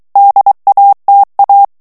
Morse code, heard as a keyed tone, about 23 wpm, 800 Hz. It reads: DATA
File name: ANGLAIS texte clair 3 vitesse 22 mots minute